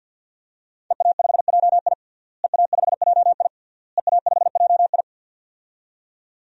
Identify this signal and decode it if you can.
Morse code, heard as a keyed tone, about 48 wpm, 700 Hz. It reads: EA5JI EA5JI EA5JI
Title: DXCC-100 Call Signs - 5 Characters - Repeated 3x - 50wpm